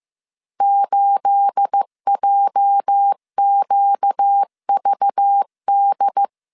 Morse code, heard as a keyed tone, about 15 wpm, 800 Hz. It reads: 8JQVD
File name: CW / DET / 1 9 1500C S C 14wpm 3245